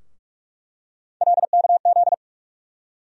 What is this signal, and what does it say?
Morse code, heard as a keyed tone, about 45 wpm, 700 Hz. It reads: LKZ
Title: USA Call Sign Postfixes - 3 Letters 45wpm